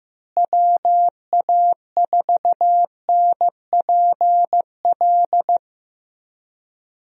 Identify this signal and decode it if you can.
Morse code, heard as a keyed tone, about 15 wpm, 700 Hz. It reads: WA4NPL
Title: USA Call Signs - 6 Characters - Repeated 3x - 15wpm